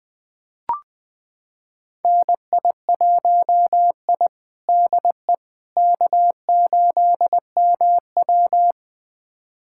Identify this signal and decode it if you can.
Morse code, heard as a keyed tone, about 20 wpm, 700 Hz. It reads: NI1I DE K8MW
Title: USA Callsign DE Callsign - 4 Characters - 20wpm